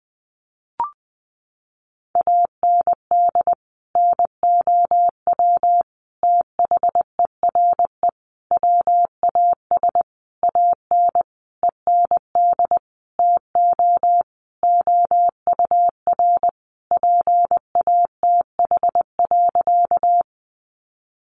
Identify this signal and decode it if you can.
Morse code, heard as a keyed tone, about 19 wpm, 700 Hz. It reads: AND NOW THERE WAS AN END TO OUR PATH.
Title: Sentences from Top 1400 Words 20wpm